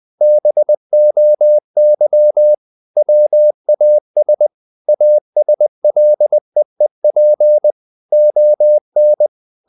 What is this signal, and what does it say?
Morse code, heard as a keyed tone, about 20 wpm, 600 Hz. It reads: BOY WAS ASLEEP ON
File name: the old man and the sea / the old man and the sea part 1 20wpm